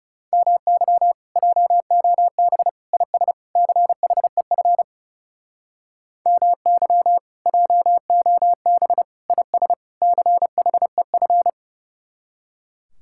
Morse code, heard as a keyed tone, about 37 wpm, 700 Hz. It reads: MY JOB IS CHEF MY JOB IS CHEF
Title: Speed-Racing Common QSO Phrases - 30wpm